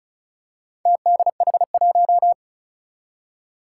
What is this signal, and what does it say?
Morse code, heard as a keyed tone, about 36 wpm, 700 Hz. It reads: TDH1
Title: Four Challenging Letter-Number Combinations 35wpm